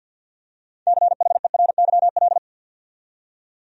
Morse code, heard as a keyed tone, about 52 wpm, 700 Hz. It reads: CHERYL
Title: USA Common Names - 50wpm